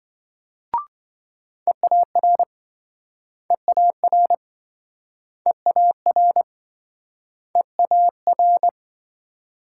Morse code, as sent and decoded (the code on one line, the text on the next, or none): . .- .-. / . .- .-. / . .- .-. / . .- .-.
EAR EAR EAR EAR